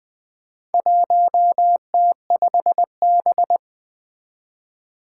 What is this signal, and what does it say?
Morse code, heard as a keyed tone, about 21 wpm, 700 Hz. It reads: 1T5B
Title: Four Challenging Letter-Number Combinations 20wpm